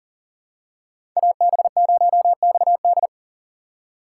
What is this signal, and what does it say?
Morse code, heard as a keyed tone, about 40 wpm, 700 Hz. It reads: AB0XD
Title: USA Call Signs - 2x Character Spacing - 40wpm